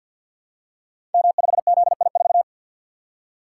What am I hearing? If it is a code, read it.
Morse code, heard as a keyed tone, about 51 wpm, 700 Hz. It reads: M5ZI4
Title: Code Groups - 5 Letter:Numbers - 50wpm